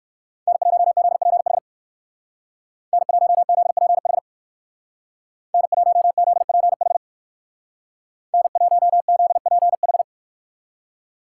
Morse code, heard as a keyed tone, about 58 wpm, 700 Hz. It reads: N1ZPH N1ZPH N1ZPH N1ZPH